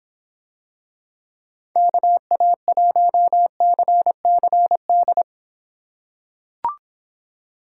Morse code, heard as a keyed tone, about 23 wpm, 700 Hz. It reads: KA1CCD